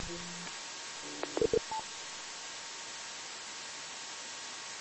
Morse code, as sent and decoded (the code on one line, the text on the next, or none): none